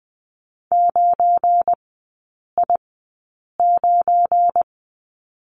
Morse code, heard as a keyed tone, about 20 wpm, spaced as for 8 wpm, 690 Hz. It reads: 9I9
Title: 172 Lesson 28e - Number 9 - Three Characters 20wpm